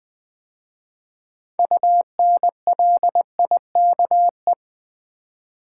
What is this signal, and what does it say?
Morse code, heard as a keyed tone, about 20 wpm, 700 Hz. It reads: UNLIKE
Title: 6 Letter Words - First 3 Letters Given - 20wpm